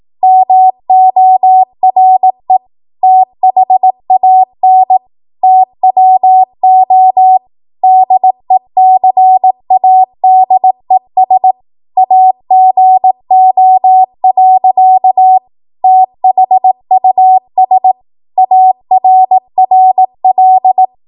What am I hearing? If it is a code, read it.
Morse code, heard as a keyed tone, about 18 wpm, 750 Hz. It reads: MORE THAN TWO DECADES AGO. THUS ARRL
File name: Morse / 18WPM